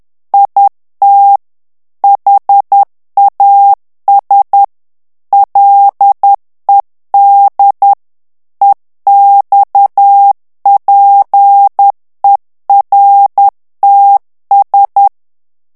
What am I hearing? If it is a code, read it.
Morse code, heard as a keyed tone, about 10 wpm, 800 Hz. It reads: IT HAS LED EXPERTS
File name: morse mp3 / ANGLAIS texte clair 2 vitesse 10 mots minute